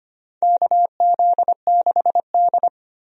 Morse code, heard as a keyed tone, about 25 wpm, 700 Hz. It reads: KZ6D
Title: USA Call Signs - 4 Characters - 25wpm